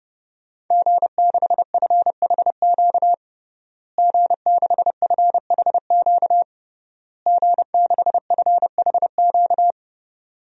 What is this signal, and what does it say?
Morse code, heard as a keyed tone, about 30 wpm, 700 Hz. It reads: G6FHQ G6FHQ G6FHQ
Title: DXCC-100 Call Signs - 5 Characters - Repeated 3x - 30wpm